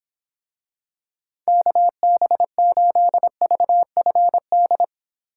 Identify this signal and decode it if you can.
Morse code, heard as a keyed tone, about 23 wpm, 700 Hz. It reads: KB8VFD